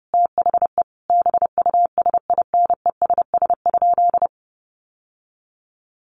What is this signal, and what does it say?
Morse code, heard as a keyed tone, about 30 wpm, 700 Hz. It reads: THE BUSINESS?
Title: Sentences from Top 1400 Words 30wpm